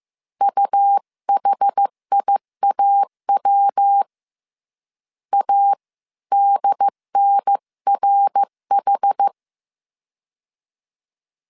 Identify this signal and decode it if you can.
Morse code, heard as a keyed tone, about 15 wpm, 800 Hz. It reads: UHIAW ADNRH
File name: CW / DET / 1 3 1500C S C 14wpm 2458